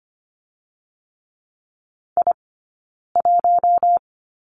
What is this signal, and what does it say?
Morse code, heard as a keyed tone, about 25 wpm, 700 Hz. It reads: I 1
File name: course / 25:10wpm / 092 Lesson 16d - Letter c - Two Characters 25wpm